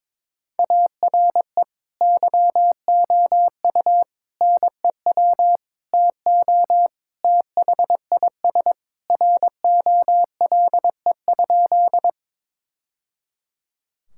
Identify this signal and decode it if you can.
Morse code, heard as a keyed tone, about 21 wpm, 700 Hz. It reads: ARE YOU NEW TO THIS ROLE?
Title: Sentences from Top 1000 Words - Review Part 6 - 22wpm